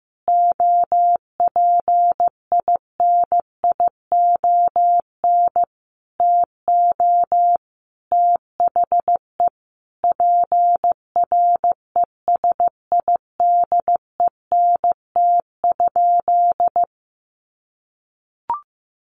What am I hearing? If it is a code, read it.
Morse code, heard as a keyed tone, about 15 wpm, 700 Hz. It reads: OPINION TO THE PRESIDENT?
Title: Sentences from Top 1300 Words 15wpm